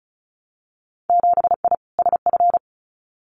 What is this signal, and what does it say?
Morse code, heard as a keyed tone, about 36 wpm, 700 Hz. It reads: 7I SF
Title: Winter Field Day Exchanges Encore 35wpm